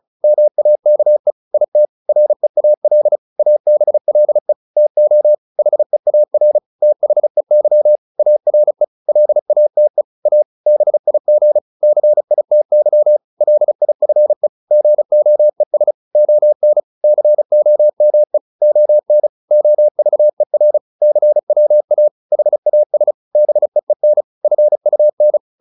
Morse code, heard as a keyed tone, about 34 wpm, 600 Hz. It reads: MAKE IT REAL ABLE TO HEAR THEY ARE LATE A BIG CITY LIFE GOES ON COME ON OVER CWA HAS BEEN FUN